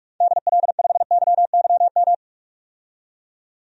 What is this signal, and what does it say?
Morse code, heard as a keyed tone, about 42 wpm, 700 Hz. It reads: DL5YYM
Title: DXCC-100 Call Signs - 6 Characters - 45wpm